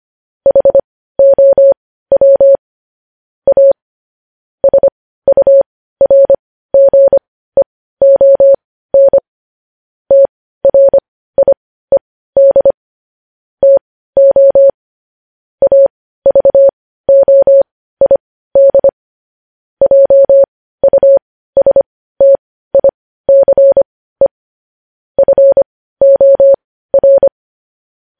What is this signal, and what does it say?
Morse code, heard as a keyed tone, about 25 wpm, spaced as for 15 wpm, 550 Hz. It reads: HOW A SURGEON TRIED TO AVOID JUSTICE FOR
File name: podcast / News headlines-1x-25x15